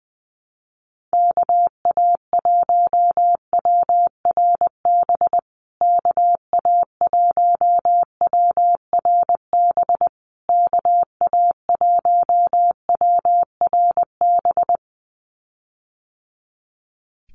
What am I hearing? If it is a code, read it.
Morse code, heard as a keyed tone, about 20 wpm, 700 Hz. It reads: KA1WRB KA1WRB KA1WRB
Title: USA Call Signs Repeated 3x 20wpm